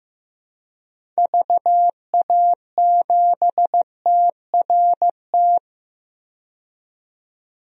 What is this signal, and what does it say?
Morse code, heard as a keyed tone, about 15 wpm, 700 Hz. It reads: VA7TRT
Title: DXCC-100 Call Signs - 6 Characters - 15wpm